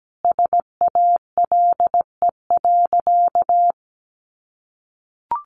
Morse code, heard as a keyed tone, about 17 wpm, 700 Hz. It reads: SALE.
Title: Sentences from Top 1400 Words 17wpm